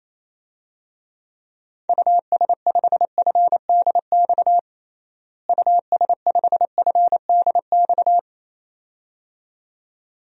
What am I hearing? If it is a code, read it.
Morse code, heard as a keyed tone, about 28 wpm, 700 Hz. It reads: US5FDX US5FDX
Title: DXCC-100 Call Signs - 6 Characters - Repeated 2x - 28wpm